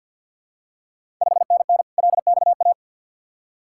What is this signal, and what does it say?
Morse code, heard as a keyed tone, about 52 wpm, 700 Hz. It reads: 5NN LXA
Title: Ontario QSO Party Exchange 50wpm